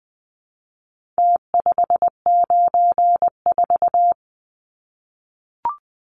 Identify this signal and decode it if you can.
Morse code, heard as a keyed tone, about 20 wpm, 700 Hz. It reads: T594
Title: Numbers - Contest Style - 4-Digit 20wpm